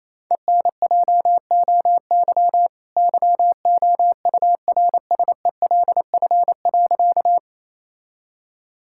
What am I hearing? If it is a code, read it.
Morse code, heard as a keyed tone, about 28 wpm, 700 Hz. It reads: ENJOY YOURSELF.
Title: Sentences from Top 1000 Words - Review Part 5 - 28wpm